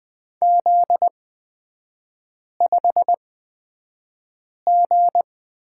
Morse code, heard as a keyed tone, about 20 wpm, spaced as for 5 wpm, 700 Hz. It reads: Z5G
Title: 243 Lesson 39e - Letter z - Three Characters 20wpm